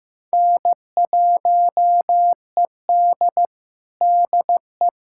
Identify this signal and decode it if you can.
Morse code, heard as a keyed tone, about 15 wpm, 700 Hz. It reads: N1ED DE
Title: USA Callsign DE Callsign - 4 Characters - 15wpm